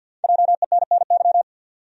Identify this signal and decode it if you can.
Morse code, heard as a keyed tone, about 52 wpm, 700 Hz. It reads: JENNY
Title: USA Common Names - Max 5 Letters - 50wpm